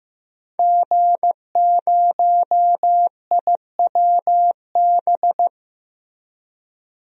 Morse code, heard as a keyed tone, about 15 wpm, 700 Hz. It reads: G0IWB
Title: DXCC-100 Call Signs - 5 Characters - Repeated 2x - 15wpm